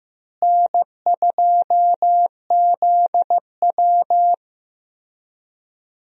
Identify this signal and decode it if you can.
Morse code, heard as a keyed tone, about 15 wpm, 700 Hz. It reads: N2ZW
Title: USA Call Signs - 4 Characters - Repeated 2x - 15wpm